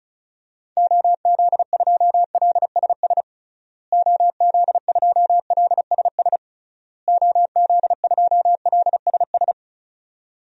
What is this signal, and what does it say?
Morse code, heard as a keyed tone, about 36 wpm, 700 Hz. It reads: OZ2LSS OZ2LSS OZ2LSS
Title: DXCC-100 Call Signs - 6 Characters - Repeated 3x - 35wpm